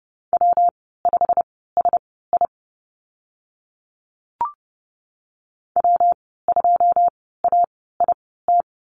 Morse code, heard as a keyed tone, about 30 wpm, spaced as for 17 wpm, 700 Hz. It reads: W5SI W2AIT
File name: Lesson 19g - Letter w - Call Signs 30wpm